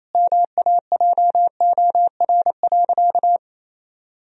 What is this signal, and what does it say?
Morse code, heard as a keyed tone, about 28 wpm, 700 Hz. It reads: MAJOR.
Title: Sentences from Top 1000 Words - Review Part 4 - 28wpm